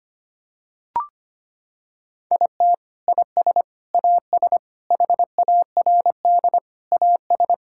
Morse code, heard as a keyed tone, about 25 wpm, 700 Hz. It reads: IT IS AS HARD AS
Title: Sentences from Top 1600 Words 25wpm